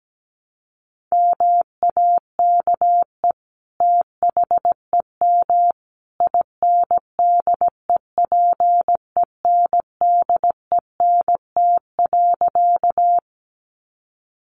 Morse code, heard as a keyed tone, about 17 wpm, 700 Hz. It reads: MAKE THEM INDEPENDENT.
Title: Sentences from Top 900 Words 17wpm